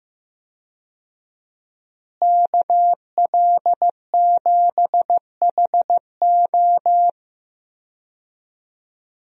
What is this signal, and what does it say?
Morse code, heard as a keyed tone, about 16 wpm, 700 Hz. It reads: KL7HO
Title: DXCC-100 Call Signs - 5 Characters - Repeated 2x - 15wpm